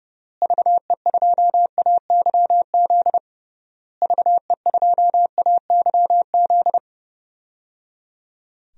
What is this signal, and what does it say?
Morse code, heard as a keyed tone, about 30 wpm, 700 Hz. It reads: VE2AYZ VE2AYZ
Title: DXCC-100 Call Signs - 6 Characters - Repeated 3x - 30wpm